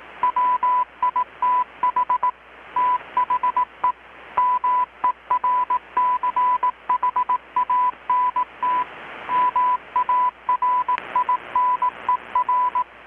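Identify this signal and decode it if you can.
Morse code, heard as a keyed tone, about 18 wpm, 1000 Hz, with radio noise. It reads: WITH THE MERCHANT MARINER
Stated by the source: Morse from KPH's Night of Nights XX maritime radiotelegraphy event, received in Chiapas, Mexico about 3,800 km away on a Kenwood R-600 with a grounded 10 m long-wire antenna.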